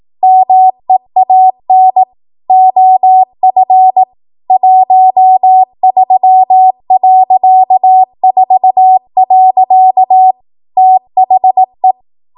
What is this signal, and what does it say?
Morse code, heard as a keyed tone, about 18 wpm, 750 Hz. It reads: MEAN OF 13.4. THE
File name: Morse / 18WPM